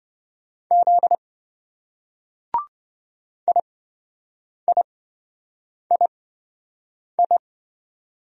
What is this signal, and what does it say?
Morse code, heard as a keyed tone, about 25 wpm, spaced as for 7 wpm, 700 Hz. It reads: Z IIII